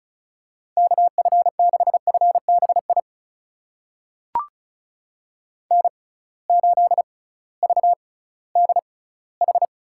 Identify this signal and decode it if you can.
Morse code, heard as a keyed tone, about 36 wpm, spaced as for 19 wpm, 700 Hz. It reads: KF6FBI N8VDH
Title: USA Call Signs - 6x Character Spacing - 35wpm